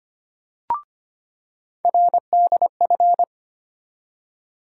Morse code, heard as a keyed tone, about 25 wpm, 700 Hz. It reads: RDF
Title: USA Call Sign Postfixes - 3 Letters 25wpm